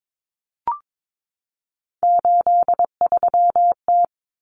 Morse code, heard as a keyed tone, about 22 wpm, 700 Hz. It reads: 83T
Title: Common Cut Numbers - 3-Digit 22wpm